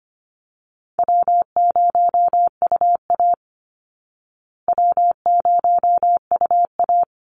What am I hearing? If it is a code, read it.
Morse code, heard as a keyed tone, about 25 wpm, 700 Hz. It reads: W0UA W0UA
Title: Easy Call Signs Repeated 2x - 25wpm